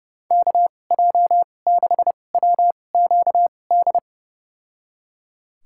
Morse code, heard as a keyed tone, about 31 wpm, spaced as for 22 wpm, 700 Hz. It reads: KJ6WQD
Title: USA Call Signs - 6 Characters - 2x Character Spacing - 30wpm